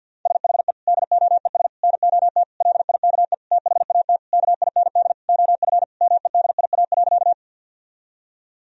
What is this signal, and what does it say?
Morse code, heard as a keyed tone, about 50 wpm, 700 Hz. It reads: SHE DOES NOT LIKE THAT KIND OF MEDIA.
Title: Sentences from Top 1000 Words - Review Part 4 - 50wpm